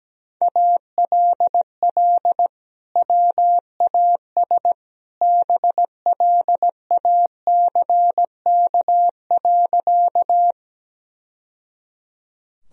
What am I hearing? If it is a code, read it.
Morse code, heard as a keyed tone, about 17 wpm, 700 Hz. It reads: ALL WAS BLACK.
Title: Sentences from Top 1000 Words - Review Part 2 - 17wpm